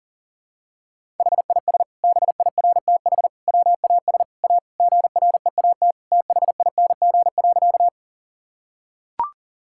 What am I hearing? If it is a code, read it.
Morse code, heard as a keyed tone, about 40 wpm, 700 Hz. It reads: HIS BIRTH WAS A GREAT THING.
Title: Sentences from Top 1000 Words - Review Part 5 - 40wpm